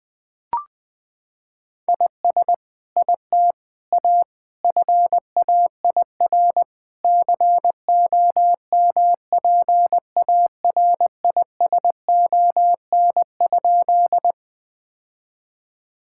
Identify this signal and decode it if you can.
Morse code, heard as a keyed tone, about 20 wpm, 700 Hz. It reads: IS IT A FAIR COMPARISON?
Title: Sentences from Top 1600 Words 20wpm